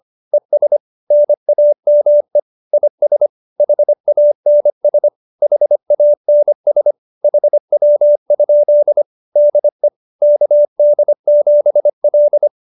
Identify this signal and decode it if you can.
Morse code, heard as a keyed tone, about 25 wpm, 570 Hz. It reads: ES NAME IS HANS HANS HW? DE KD7L